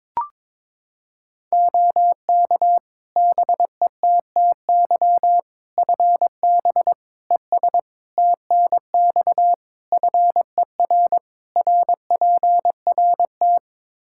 Morse code, heard as a keyed tone, about 22 wpm, 700 Hz. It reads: OK BETTY FB ES TNX FER RPRT
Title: Practice QSO - Exchange 2 - Info - 22wpm